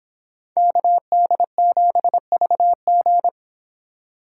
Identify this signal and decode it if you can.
Morse code, heard as a keyed tone, about 25 wpm, 700 Hz. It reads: KD7VG